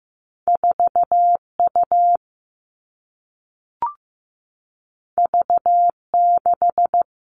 Morse code, heard as a keyed tone, about 15 wpm, 700 Hz. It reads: 4U V6
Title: Two Challenging Letter-Number Combinations 15wpm